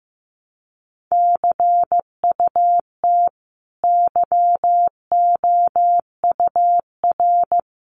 Morse code, heard as a keyed tone, about 15 wpm, 700 Hz. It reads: CUT YOUR
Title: Sentences from Top 300 Words 15wpm